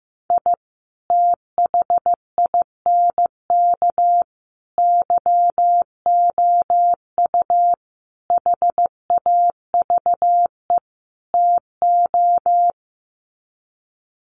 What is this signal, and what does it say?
Morse code, heard as a keyed tone, about 15 wpm, 700 Hz. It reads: I THINK YOU HAVE TO
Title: Sets of 5 Words 15wpm